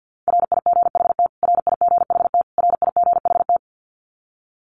Morse code, heard as a keyed tone, about 48 wpm, 700 Hz. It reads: RIGHT RIGHT RIGHT
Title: Top 200 Words Repeated 50wpm